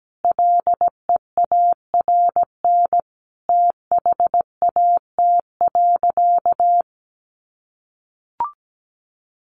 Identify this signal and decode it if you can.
Morse code, heard as a keyed tone, about 17 wpm, 700 Hz. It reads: LEARN THAT.
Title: Sentences from Top 1200 Words 17wpm